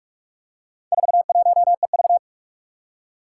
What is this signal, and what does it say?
Morse code, heard as a keyed tone, about 43 wpm, 700 Hz. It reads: 41EV
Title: Four Challenging Letter-Number Combinations 45wpm